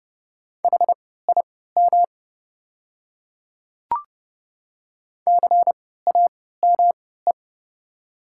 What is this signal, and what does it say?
Morse code, heard as a keyed tone, about 30 wpm, spaced as for 17 wpm, 690 Hz. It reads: HIM CAME
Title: 3x Character Spacing - Top 200 Words - 30wpm